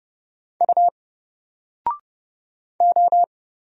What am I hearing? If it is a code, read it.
Morse code, heard as a keyed tone, about 30 wpm, 700 Hz. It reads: U O
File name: Single-Letter-Number-30wpm